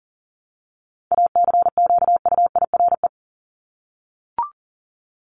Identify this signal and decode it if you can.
Morse code, heard as a keyed tone, about 40 wpm, 700 Hz. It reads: ACQUIRE
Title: Top 1501-1750 Words 40wpm